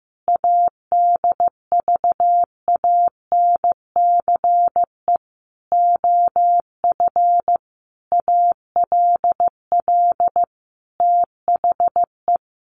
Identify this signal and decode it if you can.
Morse code, heard as a keyed tone, about 15 wpm, 700 Hz. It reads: ADVANCE OF ALL THE
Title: Sentences from Top 1400 Words 15wpm